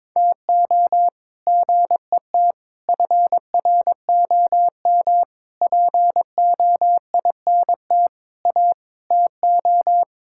Morse code, heard as a keyed tone, about 22 wpm, 700 Hz. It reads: TO GET FROM POINT A TO
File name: practice / 22wpm / Sets of 8 Words - 22wpm